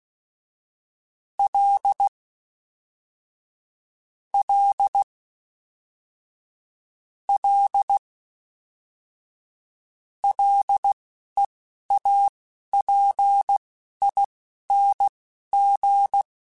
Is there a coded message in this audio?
Yes: Morse code from 1.4 to 16.2 s, heard as a keyed tone, about 16 wpm, spaced as for 12 wpm, 800 Hz. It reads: L L L LEAPING